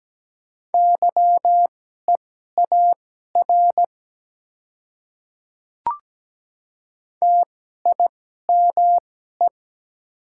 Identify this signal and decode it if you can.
Morse code, heard as a keyed tone, about 17 wpm, spaced as for 12 wpm, 700 Hz. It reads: YEAR TIME
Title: 2x Character Spacing - Top 300 Words - 17wpm